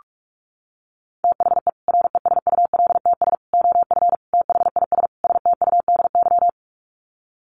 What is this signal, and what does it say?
Morse code, heard as a keyed tone, about 44 wpm, 700 Hz. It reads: THE RESULTS OF THIS STUDY
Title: Sets of 5 Words 45wpm